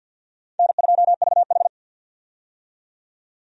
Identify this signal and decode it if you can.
Morse code, heard as a keyed tone, about 49 wpm, 700 Hz. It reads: N2VH
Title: USA Call Signs - 4 Characters - 50wpm